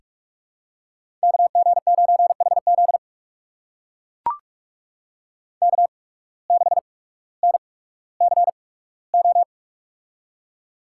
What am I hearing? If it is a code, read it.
Morse code, heard as a keyed tone, about 44 wpm, spaced as for 19 wpm, 700 Hz. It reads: KG9HZ K6NCO
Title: USA Call Signs - 5 Characters - 8x Character Spacing - 45wpm